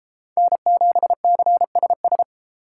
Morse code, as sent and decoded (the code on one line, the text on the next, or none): -. --... -.-. ... ...
N7CSS